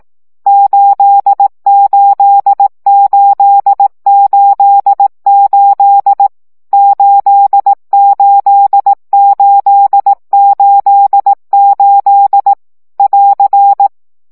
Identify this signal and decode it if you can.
Morse code, heard as a keyed tone, about 18 wpm, 800 Hz. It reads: 88888 88888 <AR>
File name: CW gyakorlas 8 15 baud